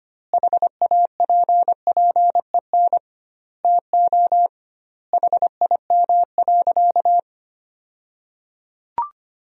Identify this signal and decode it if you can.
Morse code, heard as a keyed tone, about 25 wpm, 700 Hz. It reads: HAPPEN TO HIM.
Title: 2x Word-Spacing - Sentences from Top 400 Words - 25wpm